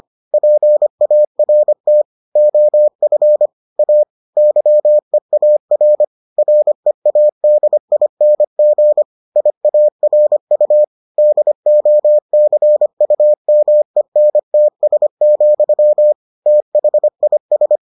Morse code, heard as a keyed tone, about 25 wpm, 600 Hz. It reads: PART OF A YEAR READING IARU DOCUMENTS, THIS